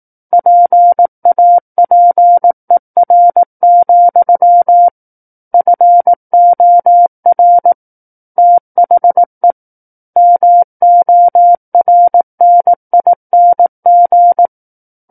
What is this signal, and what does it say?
Morse code, heard as a keyed tone, about 18 wpm, 700 Hz. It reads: PAPER, FOR THE MORNING
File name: 18 WPM